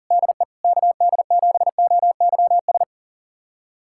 Morse code, heard as a keyed tone, about 40 wpm, 700 Hz. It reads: DE KD7OYS
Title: USA Callsign DE Callsign - 6 Characters - 40wpm